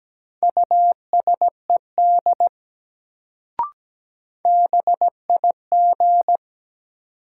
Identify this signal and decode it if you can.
Morse code, heard as a keyed tone, about 17 wpm, 700 Hz. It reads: USED BIG
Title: Sets of 2 Words - New Word Drill - 17wpm